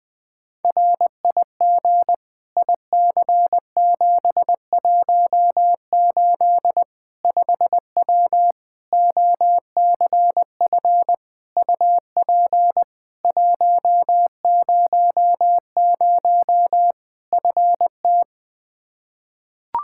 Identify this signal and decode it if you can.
Morse code, heard as a keyed tone, about 20 wpm, 700 Hz. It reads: RIG IC718 5W OCF UP 100 FT
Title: K1USN SST QSO Practice 20wpm